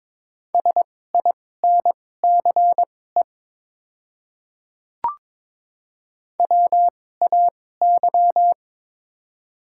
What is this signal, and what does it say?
Morse code, heard as a keyed tone, about 22 wpm, spaced as for 16 wpm, 700 Hz. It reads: SINCE WAY
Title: 2x Character Spacing - Top 300 Words - 22wpm